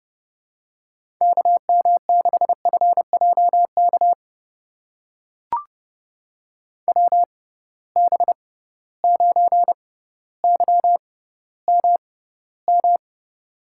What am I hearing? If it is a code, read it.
Morse code, heard as a keyed tone, about 30 wpm, spaced as for 15 wpm, 700 Hz. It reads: KM6FJK WB9YMM
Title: USA Call Signs - 6 Characters - 6x Character Spacing - 30wpm